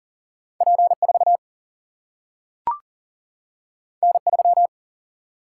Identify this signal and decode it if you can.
Morse code, heard as a keyed tone, about 40 wpm, 700 Hz. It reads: P4 N3
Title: Single-Letter Single-Number 40wpm